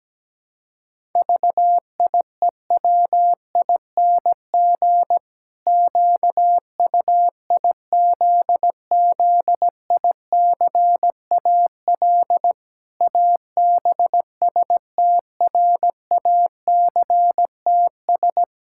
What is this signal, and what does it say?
Morse code, heard as a keyed tone, about 17 wpm, 700 Hz. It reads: VIEWING QUIZZICAL ABSTRACTS
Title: Pangram Challenge v2 - 17wpm